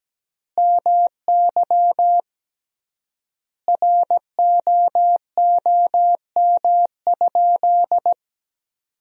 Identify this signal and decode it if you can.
Morse code, heard as a keyed tone, about 17 wpm, 700 Hz. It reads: MY ROOM?
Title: 3x Word-Spacing - Sentences from Top 300 Words - 17wpm